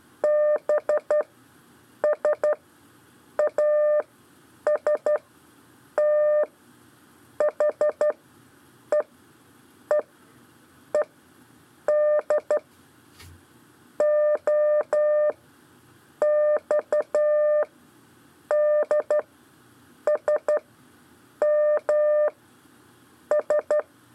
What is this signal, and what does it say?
Morse code, heard as a keyed tone, about 12 wpm, spaced as for 7 wpm, 600 Hz. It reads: BSAST HEE ED O XDSM S
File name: 4 LEZIONE CW 1